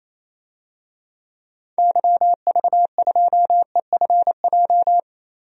Morse code, heard as a keyed tone, about 28 wpm, 700 Hz. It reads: YV2EFJ
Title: DXCC-100 Call Signs - 6 Characters - Repeated 2x - 28wpm